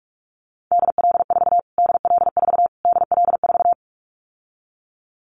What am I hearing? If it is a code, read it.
Morse code, heard as a keyed tone, about 44 wpm, 700 Hz. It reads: DL4 DL4 DL4
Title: Callsign Prefixes for Top 25 DXCC - Repeated 3x - 45wpm